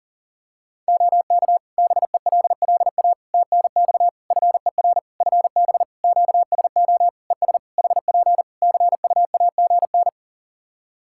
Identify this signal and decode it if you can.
Morse code, heard as a keyed tone, about 40 wpm, 700 Hz. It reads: OK BELLA TNX FER FB QSO ES HP CUAGN
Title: K1USN SST QSO Practice 40wpm